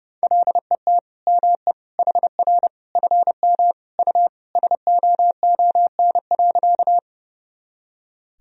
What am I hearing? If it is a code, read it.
Morse code, heard as a keyed tone, about 32 wpm, 700 Hz. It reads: LET ME HR FM U SOON.
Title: Sentences from Top 400 Words and Abbreviations 30wpm